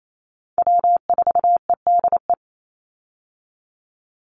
Morse code, heard as a keyed tone, about 28 wpm, 700 Hz. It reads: W4EDE
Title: Call Signs Repeated 2x - 28wpm